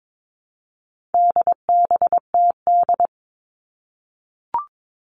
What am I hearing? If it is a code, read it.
Morse code, heard as a keyed tone, about 22 wpm, 700 Hz. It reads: DBTD
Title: All Cut Numbers - 4-Digit 22wpm